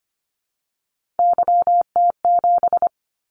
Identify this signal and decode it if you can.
Morse code, heard as a keyed tone, about 25 wpm, 700 Hz. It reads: YT7
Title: Callsign Prefixes for Top 25 DXCC - 25wpm